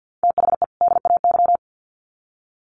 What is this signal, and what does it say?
Morse code, heard as a keyed tone, about 50 wpm, 700 Hz. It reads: THE DAY
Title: Sets of 2 Words 50wpm